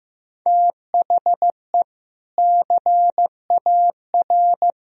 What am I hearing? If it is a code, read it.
Morse code, heard as a keyed tone, about 15 wpm, 700 Hz. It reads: THE CAR
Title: Sentences from Top 1000 Words - Review Part 1 - 15wpm